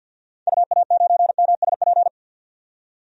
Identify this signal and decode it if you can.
Morse code, heard as a keyed tone, about 51 wpm, 700 Hz. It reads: UA9MSP
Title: DXCC-100 Call Signs - 6 Characters - 50wpm